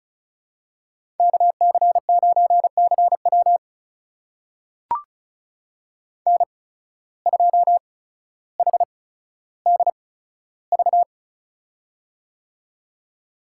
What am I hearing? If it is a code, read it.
Morse code, heard as a keyed tone, about 34 wpm, spaced as for 15 wpm, 700 Hz. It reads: KC9CW N2HDV
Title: USA Call Signs - 5 Characters - 8x Character Spacing - 35wpm